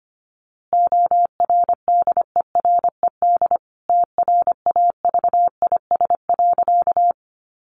Morse code, heard as a keyed tone, about 25 wpm, 700 Hz. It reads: ORDERED TRAVIS.
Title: A Sound Of Thunder 25wpm